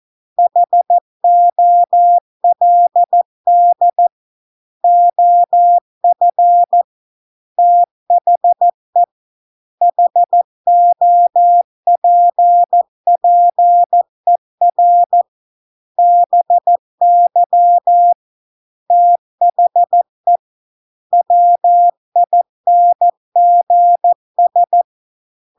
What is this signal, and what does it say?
Morse code, heard as a keyed tone, about 14 wpm, 700 Hz. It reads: HOLD OF THE HOPPER BY THE WINGS